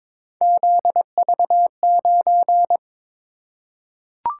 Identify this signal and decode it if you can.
Morse code, heard as a keyed tone, about 23 wpm, 700 Hz. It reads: ZV9
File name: practice / 22wpm / Three Letter-Number Combinations - 22wpm